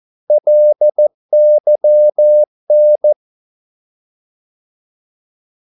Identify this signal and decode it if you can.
Morse code, heard as a keyed tone, about 14 wpm, 600 Hz. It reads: LYN
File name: Korte ord DK 14wpm